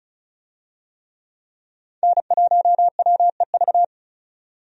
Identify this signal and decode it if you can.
Morse code, heard as a keyed tone, about 34 wpm, 700 Hz. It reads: N1WEV
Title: USA Call Signs - 5 Characters - 35wpm